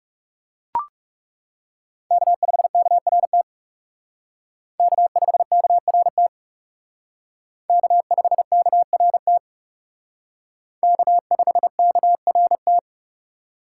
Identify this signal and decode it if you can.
Morse code, heard as a keyed tone, about 37 wpm, 700 Hz. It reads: K5KRT K5KRT K5KRT K5KRT
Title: Speed Racing - USA 1x3 Call Signs 30wpm